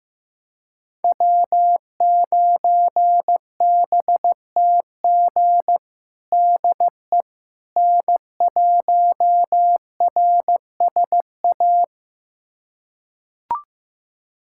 Morse code, heard as a keyed tone, about 15 wpm, 700 Hz. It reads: W9BTG DE N1RSA
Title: USA Callsign DE Callsign - 5 Characters - 15wpm